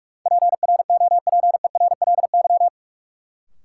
Morse code, heard as a keyed tone, about 43 wpm, 700 Hz. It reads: PROPERLY
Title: Sentences from Top 1100 Words - New Word Drill Alt - 45wpm